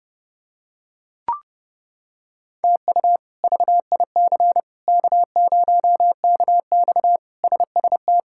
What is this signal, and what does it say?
Morse code, heard as a keyed tone, about 30 wpm, 700 Hz. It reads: TU VIC K0KX SST
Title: K1USN SST QSO Practice - 30wpm